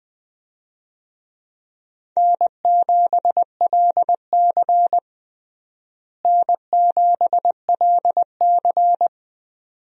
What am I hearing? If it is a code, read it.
Morse code, heard as a keyed tone, about 21 wpm, 700 Hz. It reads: N7LC N7LC
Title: USA Call Signs - 4 Characters - Repeated 2x - 20wpm